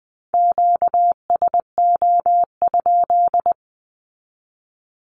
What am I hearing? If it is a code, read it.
Morse code, heard as a keyed tone, about 20 wpm, 700 Hz. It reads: QSO?
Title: Q-Codes 20wpm